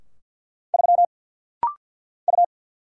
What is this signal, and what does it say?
Morse code, heard as a keyed tone, about 50 wpm, 700 Hz. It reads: 3 U
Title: Single Letter-Number Alt - Rapid-Fire 50wpm